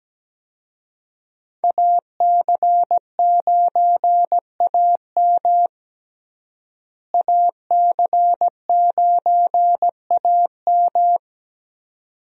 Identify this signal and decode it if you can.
Morse code, heard as a keyed tone, about 17 wpm, 700 Hz. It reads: AC9AM AC9AM
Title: USA Call Signs - 5 Characters - Repeated 2x - 17wpm